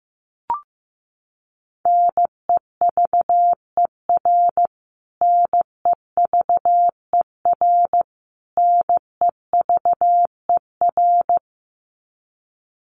Morse code, heard as a keyed tone, about 15 wpm, 700 Hz. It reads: NEVER NEVER NEVER
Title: Top 500 Words Repeated 15wpm